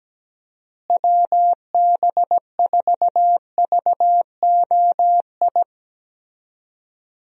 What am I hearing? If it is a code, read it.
Morse code, heard as a keyed tone, about 17 wpm, 700 Hz. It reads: WB4VOI
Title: USA Call Signs - 6 Characters - Repeated 3x - 17wpm